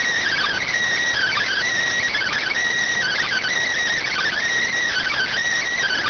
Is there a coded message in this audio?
Yes: Emergency Alert System (SAME) tones (header bursts) from 1.1 to 5.3 s, heard as harsh buzzing bursts.